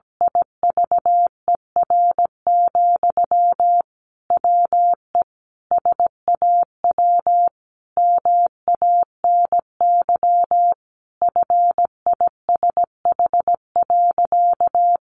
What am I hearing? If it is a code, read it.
Morse code, heard as a keyed tone, about 17 wpm, 700 Hz. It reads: IVER, WE SAW MANY FISH.
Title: Sentences from Top 1500 Words 17wpm